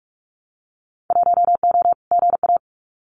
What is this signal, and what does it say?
Morse code, heard as a keyed tone, about 44 wpm, 700 Hz. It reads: 1O GA
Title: Winter Field Day Exchanges 45wpm